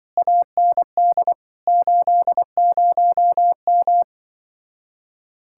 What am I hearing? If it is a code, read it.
Morse code, heard as a keyed tone, about 23 wpm, 700 Hz. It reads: AND 80M
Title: Speed-Racing Common QSO Phrases - 20wpm